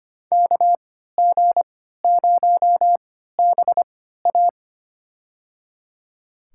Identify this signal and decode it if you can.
Morse code, heard as a keyed tone, about 25 wpm, spaced as for 14 wpm, 700 Hz. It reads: KG0BA
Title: USA Call Signs - 5 Characters - 3x Character Spacing - 25wpm